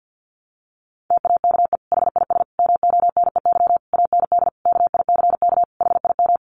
Morse code, heard as a keyed tone, about 50 wpm, 700 Hz. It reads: TAKE HIS MONEY AND KICK HIM
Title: Sentences from Top 1300 Words 50wpm